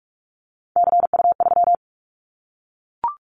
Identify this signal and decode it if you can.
Morse code, heard as a keyed tone, about 44 wpm, 700 Hz. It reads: CU3
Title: Callsign Prefixes for Top 100 DXCC - 45wpm